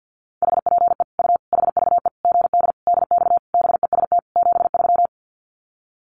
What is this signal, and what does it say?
Morse code, heard as a keyed tone, about 50 wpm, 700 Hz. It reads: HPE U HVE GD DX BEST 73
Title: Common QSO Phrases 50wpm